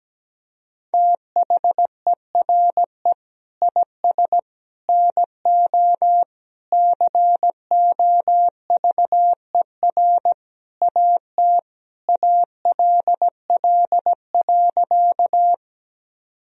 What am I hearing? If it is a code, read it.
Morse code, heard as a keyed tone, about 17 wpm, 700 Hz. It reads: THERE IS NO COVER AT ALL.
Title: Sentences from Top 1000 Words - Review Part 1 - 17wpm